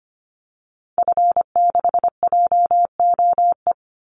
Morse code, heard as a keyed tone, about 25 wpm, 700 Hz. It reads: F6JOE
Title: Call Signs - 25wpm